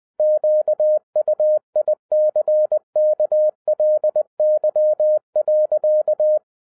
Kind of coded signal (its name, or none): Morse code